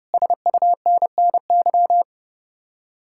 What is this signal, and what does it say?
Morse code, heard as a keyed tone, about 40 wpm, 700 Hz. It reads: SUNNY